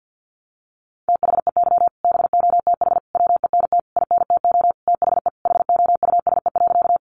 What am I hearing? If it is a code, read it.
Morse code, heard as a keyed tone, about 50 wpm, 700 Hz. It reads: THEY BOTH WENT INTO THE HOUSE.
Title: Sentences from Top 600 Words 50wpm